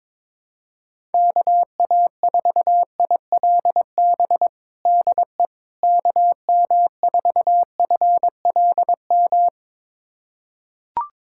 Morse code, heard as a keyed tone, about 22 wpm, 700 Hz. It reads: KA4ILB DE KM4FLM
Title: USA Callsign DE Callsign 22wpm